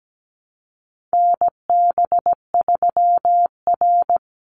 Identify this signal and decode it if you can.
Morse code, heard as a keyed tone, about 17 wpm, 700 Hz. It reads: NB3R
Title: Call Signs Repeated 2x - 17wpm